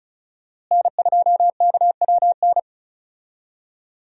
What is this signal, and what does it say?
Morse code, heard as a keyed tone, about 34 wpm, 700 Hz. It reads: N2KWN
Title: USA Call Signs - 5 Characters - Repeated 3x - 35wpm